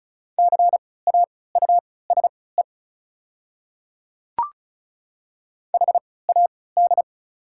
Morse code, heard as a keyed tone, about 36 wpm, spaced as for 20 wpm, 690 Hz. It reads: CAUSE HAD
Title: 3x Character Spacing - Top 200 Words - 35wpm